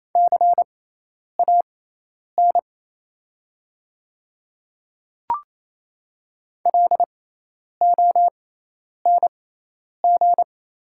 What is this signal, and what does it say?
Morse code, heard as a keyed tone, about 28 wpm, spaced as for 10 wpm, 700 Hz. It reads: CAN LONG
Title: 6x Character Spacing - Top 200 Words - 28wpm